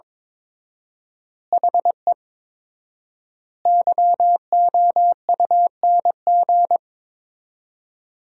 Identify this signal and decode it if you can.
Morse code, heard as a keyed tone, about 22 wpm, 700 Hz. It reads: HE YOUNG
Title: 4x Word-Spacing - Sentences from Top 400 Words - 22wpm